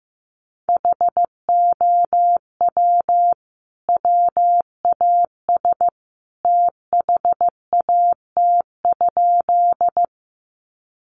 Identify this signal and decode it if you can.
Morse code, heard as a keyed tone, about 15 wpm, 700 Hz. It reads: HOW WAS THAT?
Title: Sentences from Top 100 Words 15wpm